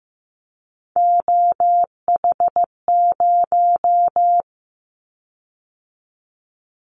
Morse code, heard as a keyed tone, about 15 wpm, 700 Hz. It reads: OH0
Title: Callsign Prefixes for Top 100 DXCC - 15wpm